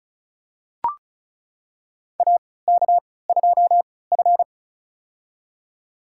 Morse code, heard as a keyed tone, about 36 wpm, spaced as for 20 wpm, 690 Hz. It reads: AK2F
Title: USA Call Signs - 4 Characters - 3x Character Spacing - 35wpm